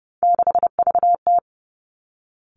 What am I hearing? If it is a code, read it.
Morse code, heard as a keyed tone, about 30 wpm, 700 Hz. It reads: 6VT
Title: Three Challenging Letter-Number Combinations 30wpm